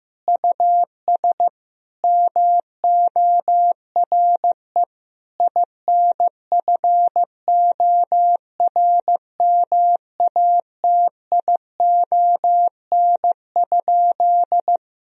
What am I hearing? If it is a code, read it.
Morse code, heard as a keyed tone, about 15 wpm, 700 Hz. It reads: US MORE INFORMATION?
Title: Sentences from Top 1000 Words - Review Part 3 - 15wpm